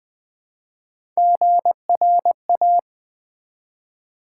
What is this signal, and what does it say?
Morse code, heard as a keyed tone, about 20 wpm, 700 Hz. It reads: GRA
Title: Build A Word - 1 Letter At A Time - 5-Letter Words - 20wpm